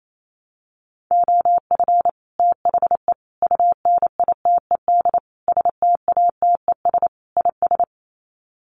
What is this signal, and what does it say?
Morse code, heard as a keyed tone, about 28 wpm, 700 Hz. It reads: OF THE UNITED STATES IS
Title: Sets of 5 Words Encore2 28wpm